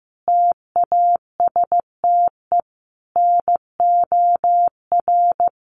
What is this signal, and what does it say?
Morse code, heard as a keyed tone, about 15 wpm, 700 Hz. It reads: TASTE NOR
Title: Sentences from Top 1500 Words 15wpm